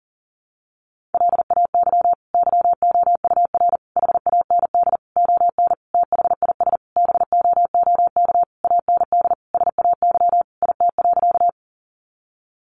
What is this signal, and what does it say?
Morse code, heard as a keyed tone, about 40 wpm, 700 Hz. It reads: LAY YOUR HAND ON THIS BOOK AND SAY IT.
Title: Sentences from Top 400 Words 40wpm